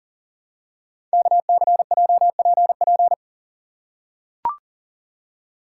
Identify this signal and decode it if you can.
Morse code, heard as a keyed tone, about 42 wpm, 690 Hz. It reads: KCJPP
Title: Code Groups - 5 Letters - 40wpm